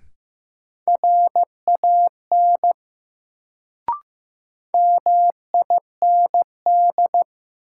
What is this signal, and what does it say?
Morse code, heard as a keyed tone, about 15 wpm, 700 Hz. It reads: RAN MIND
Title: Sentences from Top 500 Words - New Word Drill - 15wpm